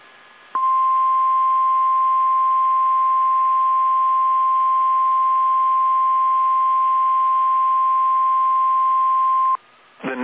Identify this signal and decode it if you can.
Emergency Alert System (SAME) tones (the 1050 Hz attention tone). It is heard from 0.5 to 9.6 s.
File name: eas / KSDZKDJL / KSDZKDJL